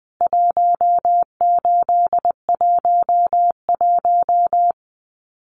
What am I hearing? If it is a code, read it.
Morse code, heard as a keyed tone, about 20 wpm, 700 Hz. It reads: 1811
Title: CWT Exchange 20wpm